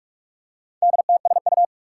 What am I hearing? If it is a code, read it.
Morse code, heard as a keyed tone, about 47 wpm, 700 Hz. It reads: DTSU
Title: Four Challenging Letter-Number Combinations 45wpm